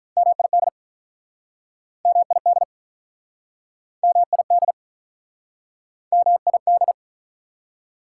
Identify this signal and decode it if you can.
Morse code, heard as a keyed tone, about 48 wpm, 700 Hz. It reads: MID MID MID MID